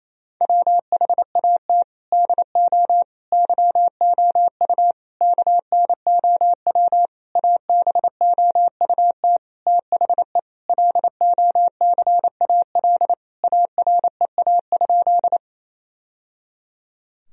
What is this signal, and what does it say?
Morse code, heard as a keyed tone, about 28 wpm, 700 Hz. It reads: WHAT DO YOU KNOW ABOUT THE LOCAL AREA?
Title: Sentences from Top 1000 Words - Review Part 3 - 28wpm